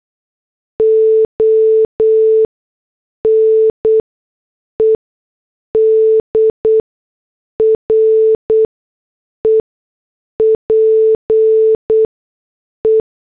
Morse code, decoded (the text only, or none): ONED REPE